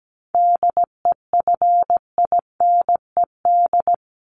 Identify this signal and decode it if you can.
Morse code, heard as a keyed tone, about 17 wpm, 700 Hz. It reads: DEFINED
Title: ED Suffix - Top 100 Words with Base Words 17wpm